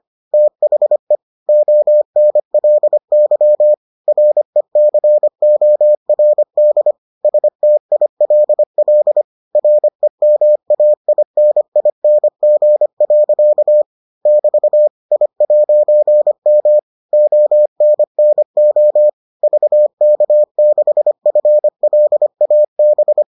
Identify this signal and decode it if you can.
Morse code, heard as a keyed tone, about 25 wpm, 600 Hz. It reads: THE ONLY RECORD STILL REMAINING. <BT> I'M ONNO VK6FLAB